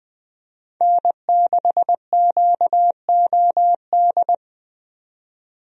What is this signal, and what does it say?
Morse code, heard as a keyed tone, about 20 wpm, 700 Hz. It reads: N6QOD
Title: USA Call Signs - 5 Characters - 20wpm